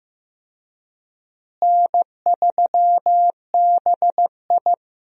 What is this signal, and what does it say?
Morse code, heard as a keyed tone, about 15 wpm, 700 Hz. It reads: N3BI
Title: USA Call Signs - 4 Characters - Repeated 2x - 15wpm